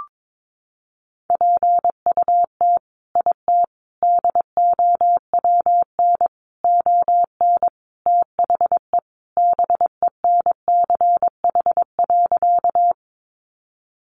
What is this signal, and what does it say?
Morse code, heard as a keyed tone, about 22 wpm, 700 Hz. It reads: PUT IT DOWN ON THE BENCH.
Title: Sentences from Top 1400 Words 22wpm